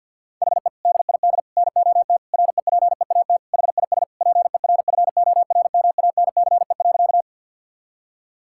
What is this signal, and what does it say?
Morse code, heard as a keyed tone, about 51 wpm, 700 Hz. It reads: HE DID NOT REPEAT HIS PERFORMANCE.
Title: Sentences from Top 1600 Words 50wpm